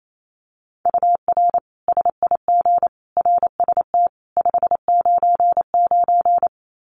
Morse code, heard as a keyed tone, about 28 wpm, 700 Hz. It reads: UR SIG RST 599
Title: Common QSO Phrases 28wpm